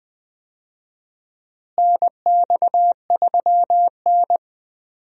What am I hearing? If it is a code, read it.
Morse code, heard as a keyed tone, about 21 wpm, 700 Hz. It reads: NX3N
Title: USA Call Signs - 4 Characters - Repeated 2x - 20wpm